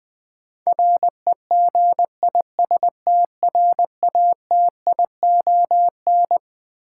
Morse code, heard as a keyed tone, about 20 wpm, 700 Hz. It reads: REGISTRATION
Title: Twelve Letter Words - 20wpm